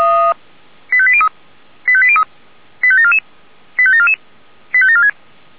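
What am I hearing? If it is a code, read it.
Five-tone selective call (ZVEI), heard as quick runs of five tones, tones of about 70 ms. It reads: 85803 85803 85744 85744 85746
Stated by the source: ZVEI five-tone call received by SDR.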